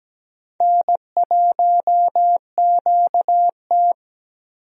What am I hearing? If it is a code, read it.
Morse code, heard as a keyed tone, about 17 wpm, 700 Hz. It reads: N1QT
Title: USA Call Signs - 4 Characters - Repeated 2x - 17wpm